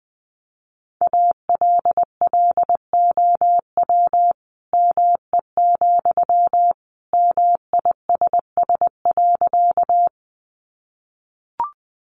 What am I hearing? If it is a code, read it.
Morse code, heard as a keyed tone, about 20 wpm, 700 Hz. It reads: ALLOW ME, MISS.
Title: Sentences from Top 700 Words 20wpm